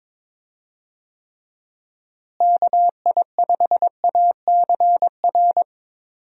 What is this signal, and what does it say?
Morse code, heard as a keyed tone, about 22 wpm, 700 Hz. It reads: KI5ACR
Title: USA Call Signs - 6 Characters - Repeated 2x - 22wpm